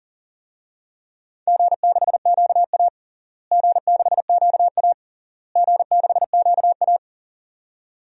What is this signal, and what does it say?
Morse code, heard as a keyed tone, about 40 wpm, 700 Hz. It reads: G6QA G6QA G6QA
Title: DXCC-100 Call Signs - 4 Characters - Repeated 3x - 40wpm